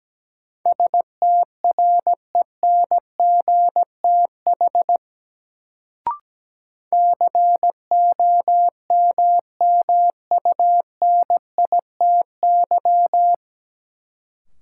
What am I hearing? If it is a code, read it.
Morse code, heard as a keyed tone, about 17 wpm, 700 Hz. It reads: STRENGTH COMMUNITY
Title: Sentences from Top 1000 Words - Review Part 4 - New Word Drill - 17wpm